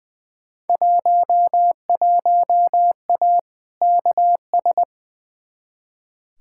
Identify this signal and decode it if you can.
Morse code, heard as a keyed tone, about 20 wpm, 700 Hz. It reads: 11A KS
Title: Field Day Exchanges Encore - 20wpm